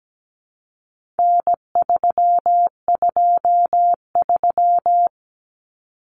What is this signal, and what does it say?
Morse code, heard as a keyed tone, about 17 wpm, 700 Hz. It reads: N323
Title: Common Cut Numbers - 4-Digit 17wpm